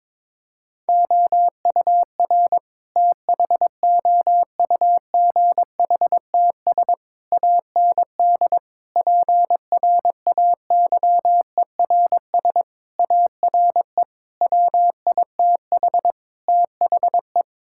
Morse code, heard as a keyed tone, about 22 wpm, 700 Hz. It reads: OUR THOUGHTS AND PRAYERS ARE WITH THE
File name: practice / 22wpm / Sets of 7 Words - 22wpm